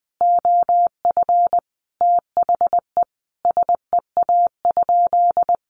Morse code, heard as a keyed tone, about 20 wpm, 700 Hz. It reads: OF THE SEA?
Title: Sentences from Top 1500 Words 20wpm